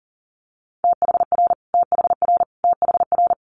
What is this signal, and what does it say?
Morse code, heard as a keyed tone, about 37 wpm, 700 Hz. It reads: THR THR THR
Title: Multi-letter Phonemes Repeated 3x 40wpm